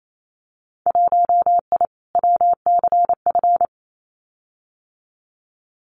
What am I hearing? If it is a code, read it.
Morse code, heard as a keyed tone, about 28 wpm, 700 Hz. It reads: 1I WCF
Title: Winter Field Day Exchanges Encore 28wpm